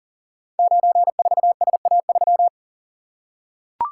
Morse code, heard as a keyed tone, about 39 wpm, 700 Hz. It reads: 94SA3
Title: Code Groups - 5 Letter:Numbers - 40wpm